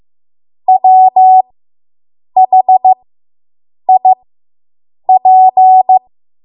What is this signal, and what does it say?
Morse code, heard as a keyed tone, about 15 wpm, spaced as for 7 wpm, 750 Hz. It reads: WHIP